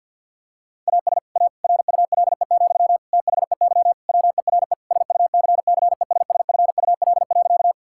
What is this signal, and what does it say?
Morse code, heard as a keyed tone, about 50 wpm, 700 Hz. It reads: AS A RULE, THEY WERE SUCCESSFUL.
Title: Sentences from Top 1000 Words - Review Part 6 - 50wpm